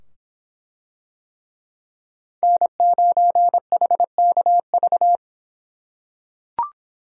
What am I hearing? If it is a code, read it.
Morse code, heard as a keyed tone, about 23 wpm, 700 Hz. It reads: N9HKV